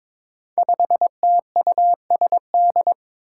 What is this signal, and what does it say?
Morse code, heard as a keyed tone, about 22 wpm, 700 Hz. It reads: 5TUSD
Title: Five Challenging Letter-Number Combinations 22wpm